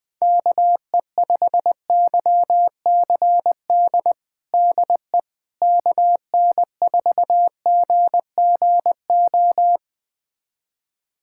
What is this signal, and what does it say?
Morse code, heard as a keyed tone, about 20 wpm, 700 Hz. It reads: KE5YCD DE KN4GGO
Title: USA Callsign DE Callsign - 6 Characters - 20wpm